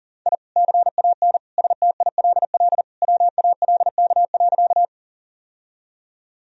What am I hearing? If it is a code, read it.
Morse code, heard as a keyed tone, about 40 wpm, 700 Hz. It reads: I CAN STILL WALK.
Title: Sentences from Top 1000 Words - Review Part 1 - 40wpm